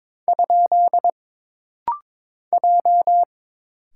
Morse code, heard as a keyed tone, about 22 wpm, 700 Hz. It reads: ? J
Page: This rapid-fire timing is geared to push your ICR (Instant Character Recognition) skills.